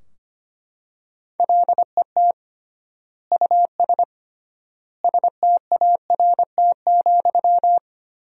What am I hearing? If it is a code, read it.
Morse code, heard as a keyed tone, about 25 wpm, 700 Hz. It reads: LET US START,
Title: 3x Word-Spacing - Sentences from Top 300 Words - 25wpm